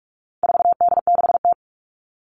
Morse code, heard as a keyed tone, about 44 wpm, 700 Hz. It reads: 4D6T
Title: All Cut Numbers - 4-Digit 45wpm